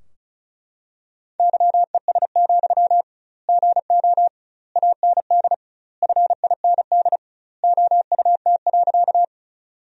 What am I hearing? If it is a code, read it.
Morse code, heard as a keyed tone, about 36 wpm, 700 Hz. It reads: YES, GO AND FIND OUT.
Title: 2x Word-Spacing - Sentences from Top 500 Words - 35wpm